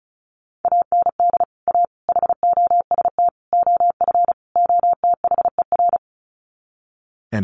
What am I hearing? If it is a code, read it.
Morse code, heard as a keyed tone, about 34 wpm, 700 Hz. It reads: AND A HOST OF OTHER
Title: Sets of 5 Words Encore2 35wpm